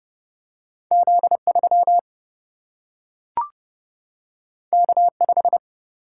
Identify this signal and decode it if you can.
Morse code, heard as a keyed tone, about 30 wpm, 700 Hz. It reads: Z3 K5
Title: Single-Letter Single-Number 30wpm